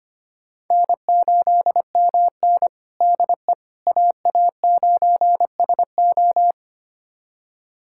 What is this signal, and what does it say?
Morse code, heard as a keyed tone, about 25 wpm, 700 Hz. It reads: N8MN DE AA9SO
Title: USA Callsign DE Callsign 25wpm